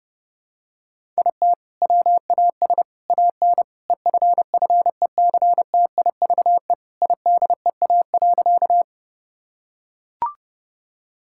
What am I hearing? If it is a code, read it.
Morse code, heard as a keyed tone, about 30 wpm, 700 Hz. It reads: IT WAS AN EFFECTIVE IDEA.
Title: Sentences from Top 1000 Words - Review Part 6 - 30wpm